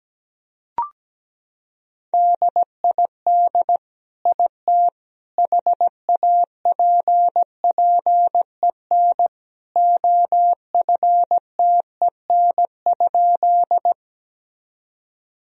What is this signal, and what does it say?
Morse code, heard as a keyed tone, about 17 wpm, 700 Hz. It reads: DID IT HAPPEN OFTEN?
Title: Sentences from Top 1000 Words - Review Part 1 - 17wpm